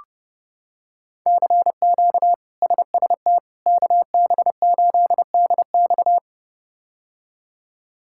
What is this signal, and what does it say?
Morse code, heard as a keyed tone, about 30 wpm, 700 Hz. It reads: CQ SST KB8DX
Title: K1USN SST QSO Practice - 30wpm